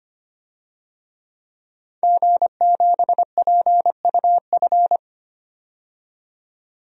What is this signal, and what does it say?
Morse code, heard as a keyed tone, about 25 wpm, 700 Hz. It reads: G7PUF
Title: DXCC-100 Call Signs - 5 Characters - 25wpm